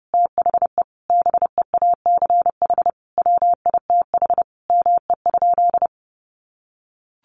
Morse code, heard as a keyed tone, about 30 wpm, 700 Hz. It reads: THE BEACH WITH ME?
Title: Sentences from Top 1500 Words 30wpm